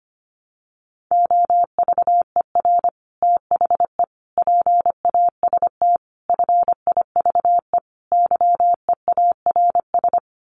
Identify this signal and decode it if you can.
Morse code, heard as a keyed tone, about 25 wpm, 690 Hz. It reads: OVER THE PAST FIVE YEARS
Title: Sets of 5 Words Encore 25wpm